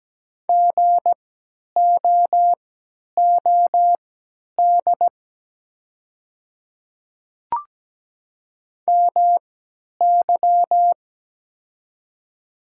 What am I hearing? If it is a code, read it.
Morse code, heard as a keyed tone, about 17 wpm, spaced as for 10 wpm, 700 Hz. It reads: GOOD MY
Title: 3x Character Spacing - Top 500 Words - 17wpm